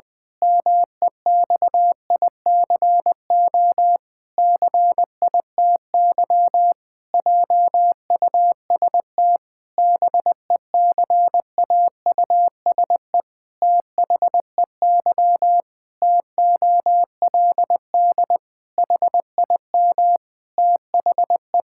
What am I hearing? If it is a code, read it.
Morse code, heard as a keyed tone, about 21 wpm, 700 Hz. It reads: MEXICO CITY JUST BECAUSE THEY TOLD HIM THE
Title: Pangram Challenge v2 - 20wpm